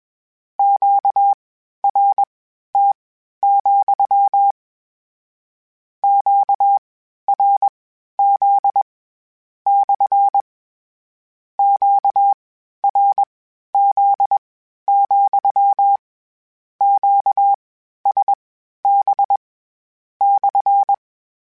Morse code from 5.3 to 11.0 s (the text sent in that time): QRZ /